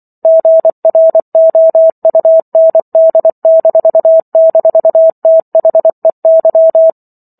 Morse code, heard as a keyed tone, about 24 wpm, 650 Hz. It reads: GROUND--THEY
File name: Chapter 15 - 24 WPM